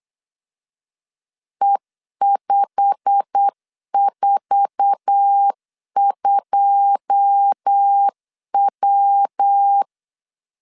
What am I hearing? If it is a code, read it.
Morse code, heard as a keyed tone, about 8 wpm, 800 Hz. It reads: E542W
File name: LEZIONI / 9 28 8wpm 3512